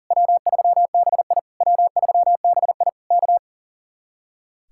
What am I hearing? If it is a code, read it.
Morse code, heard as a keyed tone, about 52 wpm, 700 Hz. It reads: W3BI W3BI K